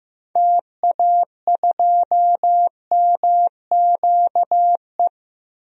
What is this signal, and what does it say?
Morse code, heard as a keyed tone, about 15 wpm, 700 Hz. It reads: TA2MQE
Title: DXCC-100 Call Signs - 6 Characters - 15wpm